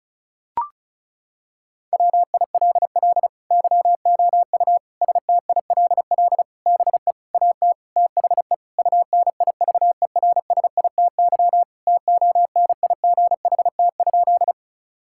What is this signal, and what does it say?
Morse code, heard as a keyed tone, about 36 wpm, 700 Hz. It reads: WILL YOU STILL BE AT THE UNIVERSITY TONIGHT?
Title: Sentences from Top 1000 Words - Review Part 5 - 35wpm